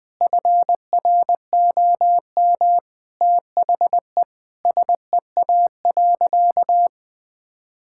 Morse code, heard as a keyed tone, about 19 wpm, 700 Hz. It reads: FROM THE SEA.
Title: Sentences from Top 1000 Words - Review Part 6 - 20wpm